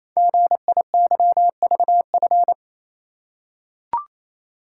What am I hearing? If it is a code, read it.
Morse code, heard as a keyed tone, about 27 wpm, 700 Hz. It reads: GIYVF
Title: Code Groups - 5 Letters - 28wpm